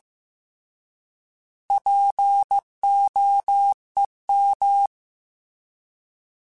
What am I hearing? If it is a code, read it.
Morse code, heard as a keyed tone, about 15 wpm, 800 Hz. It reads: POEM
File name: Alphabetical / Alphabetical mp3s / Ps @13